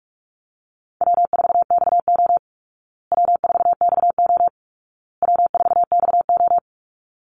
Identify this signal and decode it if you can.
Morse code, heard as a keyed tone, about 44 wpm, 700 Hz. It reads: W4XO W4XO W4XO
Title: Easy Call Signs Repeated 3x - 45wpm